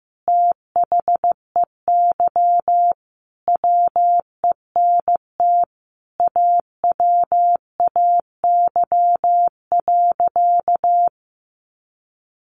Morse code, heard as a keyed tone, about 15 wpm, 700 Hz. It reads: THEY WENT AWAY.
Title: Sentences from Top 1100 Words 15wpm